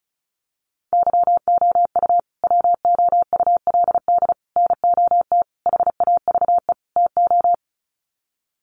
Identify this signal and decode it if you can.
Morse code, heard as a keyed tone, about 34 wpm, 700 Hz. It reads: YOU WOULD NOT HAVE TO
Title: Sets of 5 Words Encore 35wpm